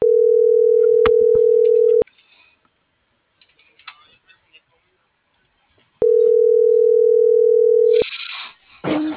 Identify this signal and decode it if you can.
Ringback tone (North America), 440 + 480 Hz.
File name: CENTRAL MAYAB / out vianey.prado